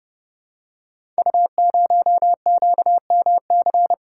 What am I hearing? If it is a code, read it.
Morse code, heard as a keyed tone, about 30 wpm, 700 Hz. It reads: U0QMC
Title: Code Groups - 5 Letter:Numbers - 30wpm